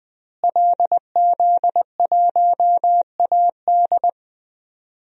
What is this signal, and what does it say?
Morse code, heard as a keyed tone, about 20 wpm, 700 Hz. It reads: LZ1AD
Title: DXCC-100 Call Signs - 5 Characters - Repeated 2x - 20wpm